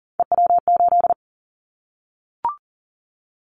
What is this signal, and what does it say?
Morse code, heard as a keyed tone, about 40 wpm, 700 Hz. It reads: EW8
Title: Callsign Prefixes for Top 100 DXCC - 40wpm